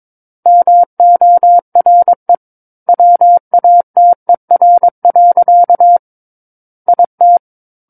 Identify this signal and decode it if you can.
Morse code, heard as a keyed tone, about 22 wpm, 700 Hz. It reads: MORE WATER. IT
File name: Book One / 13-Chapter 13 - 22 WPM